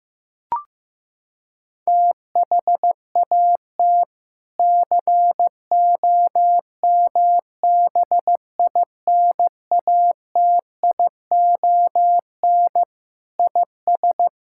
Morse code, heard as a keyed tone, about 15 wpm, 700 Hz. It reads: THAT COMBINATION IS
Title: Sentences from Top 1000 Words - Review Part 6 - 15wpm